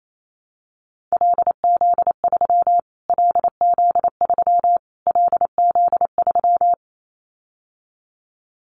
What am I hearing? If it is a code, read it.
Morse code, heard as a keyed tone, about 28 wpm, 700 Hz. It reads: LZ3 LZ3 LZ3
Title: Callsign Prefixes for Top 25 DXCC - Repeated 3x - 28wpm